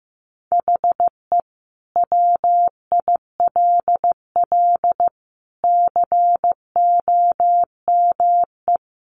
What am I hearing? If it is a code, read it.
Morse code, heard as a keyed tone, about 15 wpm, 700 Hz. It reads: HE WILL COME
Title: Sentences from Top 1400 Words 15wpm